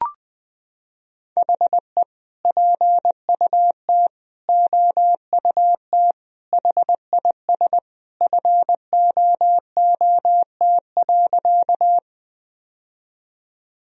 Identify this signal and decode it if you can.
Morse code, heard as a keyed tone, about 20 wpm, 700 Hz. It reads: HE PUT OUT HIS FOOT.
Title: Sentences from Top 1000 Words - Review Part 3 - 20wpm